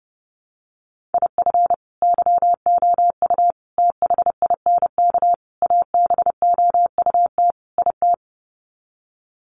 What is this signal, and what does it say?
Morse code, heard as a keyed tone, about 30 wpm, 700 Hz. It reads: IF YOU THINK ABOUT IT
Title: Sets of 5 Words Encore2 30wpm